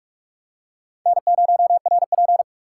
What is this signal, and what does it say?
Morse code, heard as a keyed tone, about 44 wpm, 700 Hz. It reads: N0RP
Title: USA Call Signs - 6x Character Spacing - 45wpm